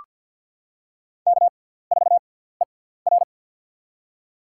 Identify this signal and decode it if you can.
Morse code, heard as a keyed tone, about 48 wpm, spaced as for 17 wpm, 700 Hz. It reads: K4ER
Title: USA Call Signs - 4 Characters - 6x Character Spacing - 50wpm